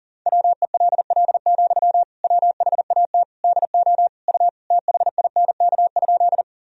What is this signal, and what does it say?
Morse code, heard as a keyed tone, about 40 wpm, 700 Hz. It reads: WELL, WHAT DO U THINK?
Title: Sentences from Top 200 Words and Abbreviations 40wpm